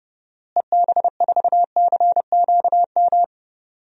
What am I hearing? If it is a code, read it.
Morse code, heard as a keyed tone, about 32 wpm, 700 Hz. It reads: EB4CQM
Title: DXCC-100 Call Signs - 6 Characters - Repeated 3x - 30wpm